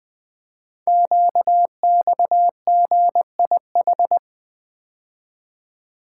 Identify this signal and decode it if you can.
Morse code, heard as a keyed tone, about 20 wpm, 700 Hz. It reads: QXGIH
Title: Code Groups - 5 Letters - 20wpm